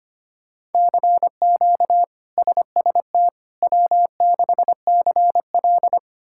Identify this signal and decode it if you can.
Morse code, heard as a keyed tone, about 25 wpm, 700 Hz. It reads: CQ SST W6CL
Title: K1USN SST QSO Practice - 25wpm